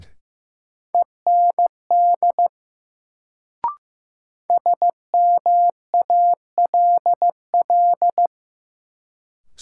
Morse code, heard as a keyed tone, about 15 wpm, 700 Hz. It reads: END SMALL
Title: Sentences from Top 200 Words - New Word Drill - 15wpm